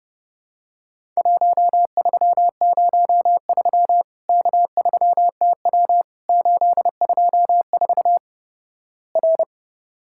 Morse code, heard as a keyed tone, about 32 wpm, 700 Hz. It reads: 1303 K3TW 824 R
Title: Worked All Europe DX Contest - QTC Exchange - 30wpm